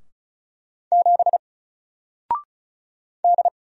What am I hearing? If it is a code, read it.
Morse code, heard as a keyed tone, about 36 wpm, 700 Hz. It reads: 7 D
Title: Single Character (Letter:Number:Punctuation:Prosign) 35wpm